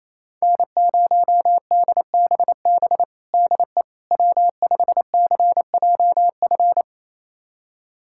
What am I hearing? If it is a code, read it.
Morse code, heard as a keyed tone, about 28 wpm, 700 Hz. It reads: N0DBB DE W5CJF
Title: USA Callsign DE Callsign - 5 Characters - 28wpm